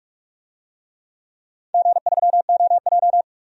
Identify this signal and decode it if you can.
Morse code, heard as a keyed tone, about 44 wpm, 700 Hz. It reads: G3OJ
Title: DXCC-100 Call Signs - 4 Characters - 45wpm